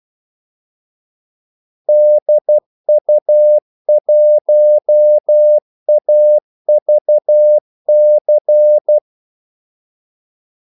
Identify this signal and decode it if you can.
Morse code, heard as a keyed tone, about 12 wpm, 600 Hz. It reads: DU1AVC
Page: 2 ekstra ordmellemrum mellem kaldesignalerne